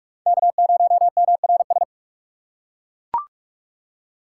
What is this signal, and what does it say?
Morse code, heard as a keyed tone, about 40 wpm, 700 Hz. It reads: K0MRS